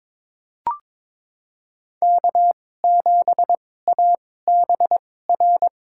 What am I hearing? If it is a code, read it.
Morse code, heard as a keyed tone, about 22 wpm, spaced as for 16 wpm, 700 Hz. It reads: K7ABR
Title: USA Call Signs - 5 Characters - 2x Character Spacing - 22wpm